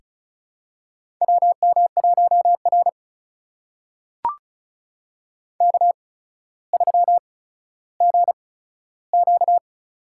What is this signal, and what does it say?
Morse code, heard as a keyed tone, about 36 wpm, spaced as for 15 wpm, 700 Hz. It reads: WM1R K3GQ
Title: USA Call Signs - 4 Characters - 8x Character Spacing - 35wpm